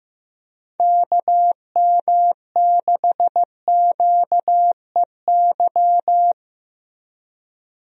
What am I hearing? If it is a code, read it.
Morse code, heard as a keyed tone, about 15 wpm, 700 Hz. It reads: KM6QEY
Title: USA Call Signs - 6 Characters - Repeated 2x - 15wpm